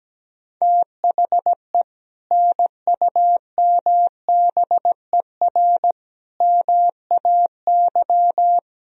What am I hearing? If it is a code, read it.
Morse code, heard as a keyed tone, about 17 wpm, 700 Hz. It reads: THE NUMBER MAY
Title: Sentences from Top 1000 Words - Review Part 4 - 17wpm